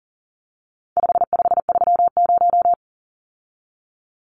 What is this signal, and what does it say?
Morse code, heard as a keyed tone, about 40 wpm, 700 Hz. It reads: 5530
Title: Numbers - 4-Digit 40wpm